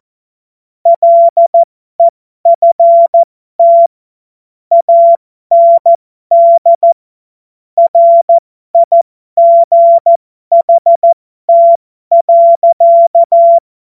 Morse code, heard as a keyed tone, about 14 wpm, spaced as for 12 wpm, 680 Hz. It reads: LEFT AND RIGHT.
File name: Morsekurs / Witze